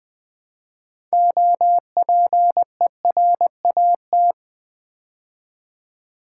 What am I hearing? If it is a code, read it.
Morse code, heard as a keyed tone, about 20 wpm, 700 Hz. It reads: OPERAT
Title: Build A Word - 2 Letters At A Time - 8-Letter Words - 20wpm